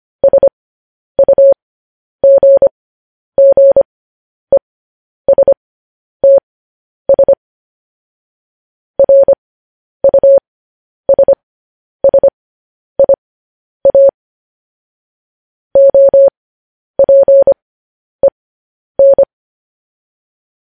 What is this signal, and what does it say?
Morse code, heard as a keyed tone, about 25 wpm, spaced as for 10 wpm, 550 Hz. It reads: SUGGESTS RUSSIA OPEN
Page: News Headlines in Morse Code at 10 WPM